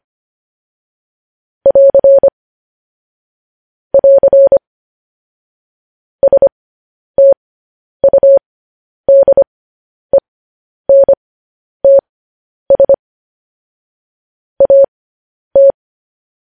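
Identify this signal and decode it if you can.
Morse code, heard as a keyed tone, about 25 wpm, spaced as for 10 wpm, 550 Hz. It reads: <AR> <AR> STUDENTS AT
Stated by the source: Morse code news headlines at 10.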